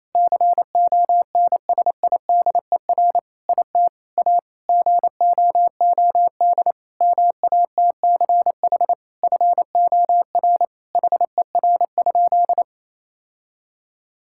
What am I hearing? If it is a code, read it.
Morse code, heard as a keyed tone, about 27 wpm, 700 Hz. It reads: CONSIDER IT A GOOD MATCH FOR HER?
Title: Sentences from Top 1600 Words 28wpm